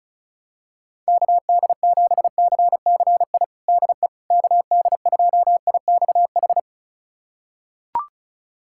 Morse code, heard as a keyed tone, about 36 wpm, 700 Hz. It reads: KD7CCI DE KD2IXH
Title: USA Callsign DE Callsign - 6 Characters - 35wpm